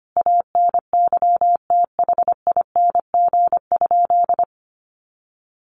Morse code, heard as a keyed tone, about 25 wpm, 700 Hz. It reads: ANYTHING?
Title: Sentences from Top 1200 Words 25wpm